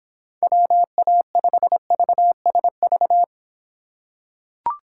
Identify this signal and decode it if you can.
Morse code, heard as a keyed tone, about 25 wpm, 700 Hz. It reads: WA5VSV